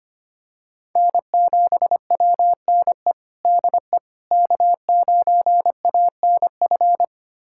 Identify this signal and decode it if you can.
Morse code, heard as a keyed tone, about 25 wpm, 700 Hz. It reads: N7WNE DE K9ANF
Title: USA Callsign DE Callsign - 5 Characters - 25wpm